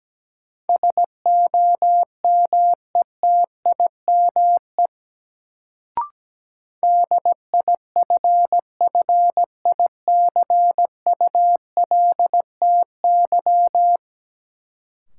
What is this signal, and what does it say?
Morse code, heard as a keyed tone, about 17 wpm, 700 Hz. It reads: SOMETIME DIFFICULTY
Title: Sentences from Top 1500 Words - New Word Drill Alt - 17wpm